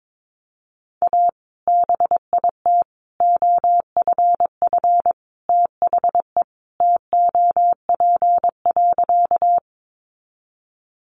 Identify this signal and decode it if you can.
Morse code, heard as a keyed tone, about 22 wpm, 700 Hz. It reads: A BIT OFF THE TOP.
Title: Sentences from Top 900 Words 22wpm